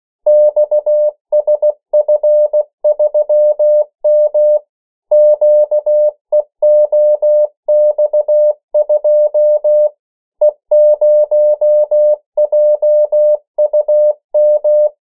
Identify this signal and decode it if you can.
Morse code, heard as a keyed tone, about 16 wpm, 600 Hz. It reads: XSF3M QEOX2 E0JUM
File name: morse mp3 / 14 urok 13